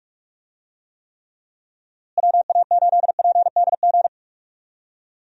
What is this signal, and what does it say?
Morse code, heard as a keyed tone, about 46 wpm, 700 Hz. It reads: WA8PDG
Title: USA Call Signs - 6 Characters - Repeated 3x - 45wpm